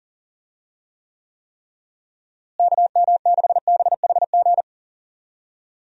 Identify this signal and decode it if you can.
Morse code, heard as a keyed tone, about 40 wpm, 700 Hz. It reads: KM6BHG
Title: USA Call Signs - 6 Characters - Repeated 3x - 40wpm